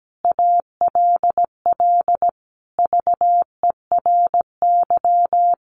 Morse code, heard as a keyed tone, about 17 wpm, 700 Hz. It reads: ALL VERY
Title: Sets of 5 Words Encore 17wpm